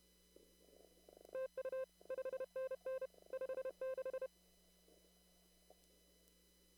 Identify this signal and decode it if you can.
Morse code, heard as a keyed tone, about 31 wpm, 520 Hz. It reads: TU 5NN 56
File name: CW / kt5j wpx cw-op-k5pi